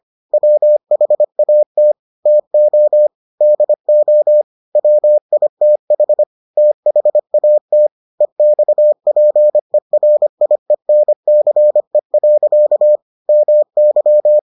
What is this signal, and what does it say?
Morse code, heard as a keyed tone, about 25 wpm, 600 Hz. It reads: WHAT TO DO WITH THAT EXPERIENCE. MY